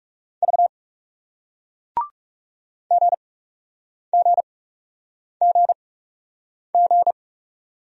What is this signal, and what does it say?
Morse code, heard as a keyed tone, about 40 wpm, spaced as for 8 wpm, 700 Hz. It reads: V GGGG